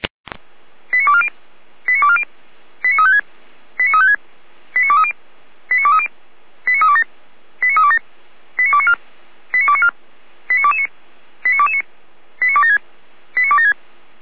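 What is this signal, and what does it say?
Five-tone selective call (ZVEI), heard as quick runs of five tones, tones of about 70 ms. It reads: 89240 89240 89356 89356 89230 89230 89327 89327 89294 89294 89229 89229 89276 89276
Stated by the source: ZVEI five-tone selective call received via SDR.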